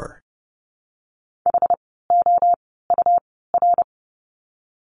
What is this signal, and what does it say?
Morse code, heard as a keyed tone, about 30 wpm, spaced as for 17 wpm, 700 Hz. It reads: HOUR
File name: course / 30:10wpm / 108 Lesson 18f - Letter m - Words 30wpm